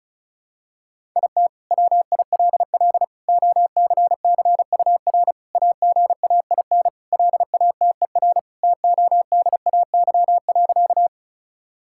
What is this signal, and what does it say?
Morse code, heard as a keyed tone, about 34 wpm, 700 Hz. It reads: IT WILL OCCUR AGAIN LATER TODAY.
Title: Sentences from Top 1000 Words - Review Part 6 - 35wpm